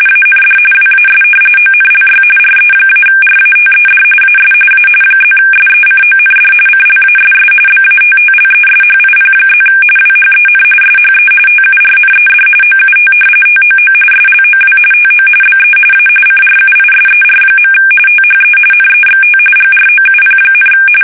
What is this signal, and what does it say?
RTTY, about 45.45 baud. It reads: AND WAS ONLY RE-RATED AS A +SECOND CLASS BATTLESHIP+ WHEN SHE TURNED OUT TOO SLOW TO BE A CRUISER. THE NEXT GROUP,
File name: fsk850 / battleships